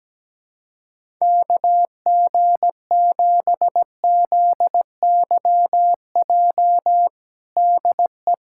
Morse code, heard as a keyed tone, about 17 wpm, 700 Hz. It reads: KG7ZYJ DE
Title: USA Callsign DE Callsign - 6 Characters - 17wpm